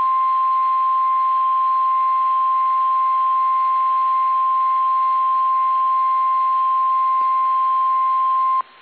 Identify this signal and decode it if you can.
Emergency Alert System (SAME) tones (the 1050 Hz attention tone). It is heard from 0.9 to 7.8 s.